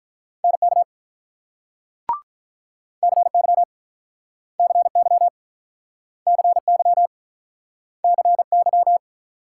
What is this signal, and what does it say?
Morse code, heard as a keyed tone, about 46 wpm, 700 Hz. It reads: NK CY CY CY CY